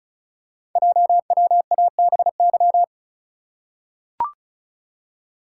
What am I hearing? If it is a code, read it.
Morse code, heard as a keyed tone, about 36 wpm, 690 Hz. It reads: JWABY
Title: Code Groups - 5 Letters - 35wpm